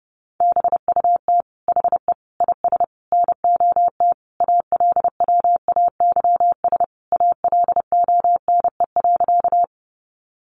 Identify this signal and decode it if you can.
Morse code, heard as a keyed tone, about 30 wpm, 700 Hz. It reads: BUT HE IS NOT ALWAYS ALONE.
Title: Sentences from Top 1000 Words 30wpm